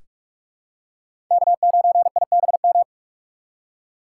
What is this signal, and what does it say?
Morse code, heard as a keyed tone, about 42 wpm, 700 Hz. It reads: K9IBM
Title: K1USN SST QSO Practice - 45wpm